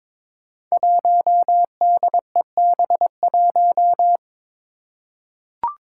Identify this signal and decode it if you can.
Morse code, heard as a keyed tone, about 22 wpm, 700 Hz. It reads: 1DEB1
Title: Five Challenging Letter-Number Combinations 22wpm